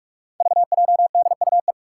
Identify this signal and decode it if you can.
Morse code, heard as a keyed tone, about 43 wpm, 700 Hz. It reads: VJDUE
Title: Five Challenging Letter-Number Combinations 45wpm